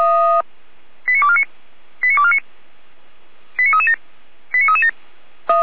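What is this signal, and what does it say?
Five-tone selective call (ZVEI), heard as quick runs of five tones, tones of about 70 ms. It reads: 89249 89249 89337 89337
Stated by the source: ZVEI five-tone call received by SDR.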